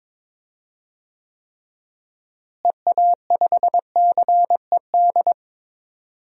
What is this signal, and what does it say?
Morse code, heard as a keyed tone, about 22 wpm, 700 Hz. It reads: EA5CED
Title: DXCC-100 Call Signs - 6 Characters - Repeated 3x - 22wpm